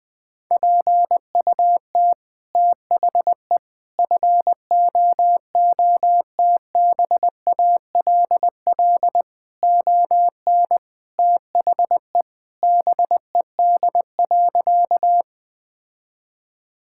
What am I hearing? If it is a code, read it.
Morse code, heard as a keyed tone, about 20 wpm, 700 Hz. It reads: PUT THE FOOTBALL ON THE BED.
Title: Sentences from Top 1600 Words 20wpm